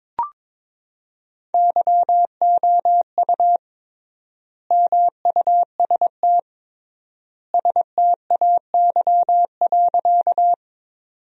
Morse code, heard as a keyed tone, about 22 wpm, 700 Hz. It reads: YOU MUST STAY.
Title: 3x Word-Spacing - Sentences from Top 500 Words - 22wpm